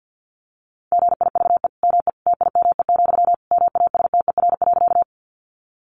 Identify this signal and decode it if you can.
Morse code, heard as a keyed tone, about 50 wpm, 700 Hz. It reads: GIVE ME TIME, MASTER.
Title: Sentences from Top 1300 Words 50wpm